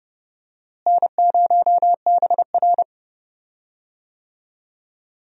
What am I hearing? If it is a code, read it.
Morse code, heard as a keyed tone, about 30 wpm, 700 Hz. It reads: N0BR
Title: USA Call Signs - 4 Characters - Repeated 3x - 30wpm